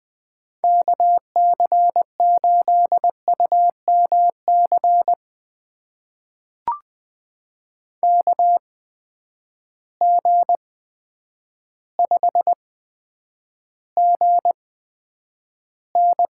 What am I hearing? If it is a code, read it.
Morse code, heard as a keyed tone, about 20 wpm, spaced as for 9 wpm, 700 Hz. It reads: KC8UMC KG5GN
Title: USA Call Signs - 6 Characters - 8x Character Spacing - 20wpm